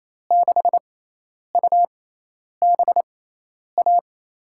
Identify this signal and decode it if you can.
Morse code, heard as a keyed tone, about 28 wpm, spaced as for 10 wpm, 700 Hz. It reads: 6UBA
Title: USA Call Signs - 6 Characters - 6x Character Spacing - 28wpm